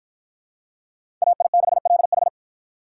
Morse code, heard as a keyed tone, about 52 wpm, 700 Hz. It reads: AI6LH